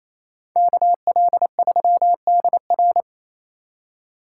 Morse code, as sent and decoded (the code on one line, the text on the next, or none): -.- .-.. ...-- -.. .-.
KL3DR